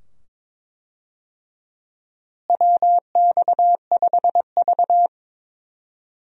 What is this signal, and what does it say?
Morse code, heard as a keyed tone, about 22 wpm, 700 Hz. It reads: WX5V
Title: USA Call Signs - 4 Characters - Repeated 2x - 22wpm